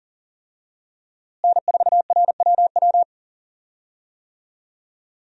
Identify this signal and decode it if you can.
Morse code, heard as a keyed tone, about 40 wpm, 700 Hz. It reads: N4RWW
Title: USA Call Signs - 5 Characters - Repeated 2x - 40wpm